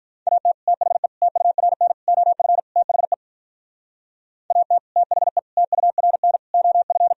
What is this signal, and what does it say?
Morse code, heard as a keyed tone, about 45 wpm, 700 Hz. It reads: AT THE TURN OF THE AT THE TURN OF